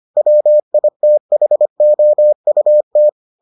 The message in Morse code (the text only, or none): WITHOUT